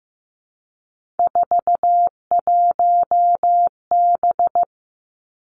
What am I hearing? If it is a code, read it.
Morse code, heard as a keyed tone, about 15 wpm, 700 Hz. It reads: 41B
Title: Three Challenging Letter-Number Combinations 15wpm